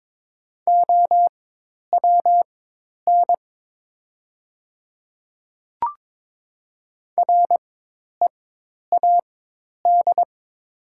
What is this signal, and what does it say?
Morse code, heard as a keyed tone, about 22 wpm, spaced as for 10 wpm, 700 Hz. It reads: OWN READ
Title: 4x Character Spacing - Top 300 Words - 22wpm